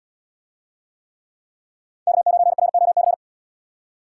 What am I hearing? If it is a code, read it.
Morse code, heard as a keyed tone, about 76 wpm, 700 Hz. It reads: K9ROC